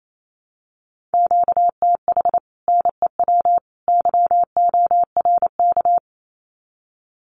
Morse code, heard as a keyed tone, about 28 wpm, 700 Hz. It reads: QTH NEW YORK
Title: Common QSO Phrases 28wpm